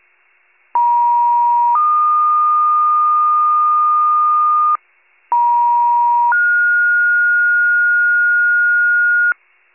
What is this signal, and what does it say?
Two-tone page, heard as a one-second tone, then a three-second tone, about 950 then 1220 Hz, about 950 then 1430 Hz, with noise. It is heard from 0.7 to 9.3 s.
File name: pager / 83 RESCUE OFC / wcalert